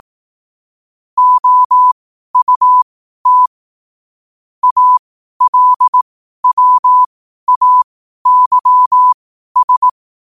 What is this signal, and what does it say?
Morse code, heard as a keyed tone, about 18 wpm, spaced as for 12 wpm, 1000 Hz. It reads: OUT ALWAYS
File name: qotdfiles / 12wpm